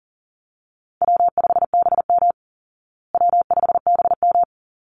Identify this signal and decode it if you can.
Morse code, heard as a keyed tone, about 40 wpm, 700 Hz. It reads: W5BM W5BM
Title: Easy Call Signs Repeated 2x - 40wpm